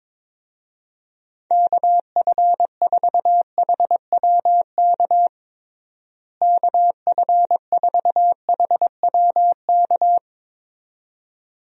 Morse code, heard as a keyed tone, about 22 wpm, 700 Hz. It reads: KF4HWK KF4HWK
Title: USA Call Signs - 6 Characters - Repeated 2x - 22wpm